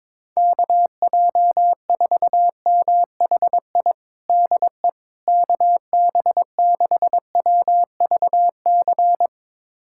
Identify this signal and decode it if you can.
Morse code, heard as a keyed tone, about 22 wpm, 700 Hz. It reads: KJ4MHI DE KB6WVC
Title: USA Callsign DE Callsign - 6 Characters - 22wpm